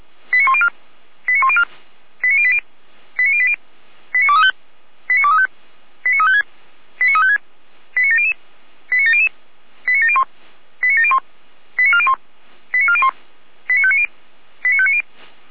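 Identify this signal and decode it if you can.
Five-tone selective call (ZVEI), heard as quick runs of five tones, tones of about 70 ms. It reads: 89104 89104 89080 89080 89235 89235 89356 89356 89700 89700 89701 89701 89401 89401 89590 89590
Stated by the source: ZVEI five-tone call received by SDR.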